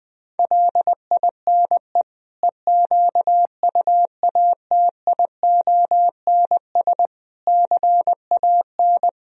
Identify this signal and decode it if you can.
Morse code, heard as a keyed tone, about 20 wpm, 700 Hz. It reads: LINE EQUATIONS CAN
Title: Speed-Racing - Sets of 5 Words 17wpm